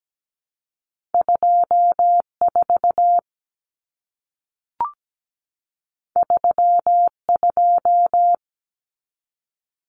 Morse code, heard as a keyed tone, about 17 wpm, 700 Hz. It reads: 24 32
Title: Numbers - 2-Digit 17wpm